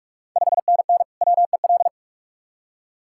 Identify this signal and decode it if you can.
Morse code, heard as a keyed tone, about 46 wpm, 700 Hz. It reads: 5NN WEL
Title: Ontario QSO Party Exchange 45wpm